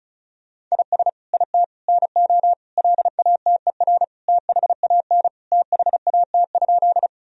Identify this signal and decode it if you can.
Morse code, heard as a keyed tone, about 34 wpm, 700 Hz. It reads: IS IT NO LATER THAN THAT?
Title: Sentences from Top 1000 Words - Review Part 4 - 35wpm